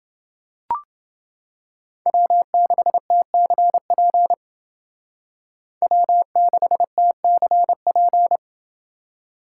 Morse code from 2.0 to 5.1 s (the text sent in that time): W6TCP